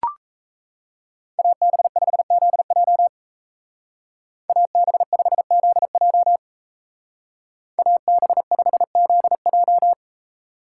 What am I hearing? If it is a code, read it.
Morse code, heard as a keyed tone, about 33 wpm, 700 Hz. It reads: AB5ZJ AB5ZJ AB5ZJ
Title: Speed Racing - USA 2x2 Call Signs 28wpm